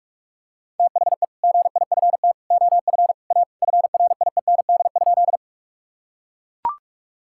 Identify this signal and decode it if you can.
Morse code, heard as a keyed tone, about 41 wpm, 700 Hz. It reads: THE GIFT OF A FRIEND?
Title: Sentences from Top 1000 Words - Review Part 6 - 45wpm